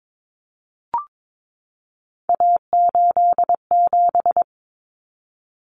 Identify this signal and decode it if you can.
Morse code, heard as a keyed tone, about 22 wpm, 700 Hz. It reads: A87
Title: Common Cut Numbers - 3-Digit 22wpm